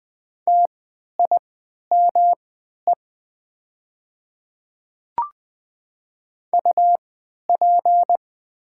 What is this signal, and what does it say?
Morse code, heard as a keyed tone, about 20 wpm, spaced as for 11 wpm, 700 Hz. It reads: TIME UP
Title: 3x Character Spacing - Top 100 Words - 20wpm